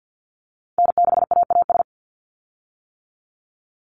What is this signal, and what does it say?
Morse code, heard as a keyed tone, about 50 wpm, 700 Hz. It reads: N6AAS
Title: USA Call Signs 50wpm